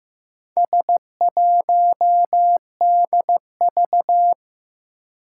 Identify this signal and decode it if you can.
Morse code, heard as a keyed tone, about 15 wpm, 700 Hz. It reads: S1DV
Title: Four Challenging Letter-Number Combinations 15wpm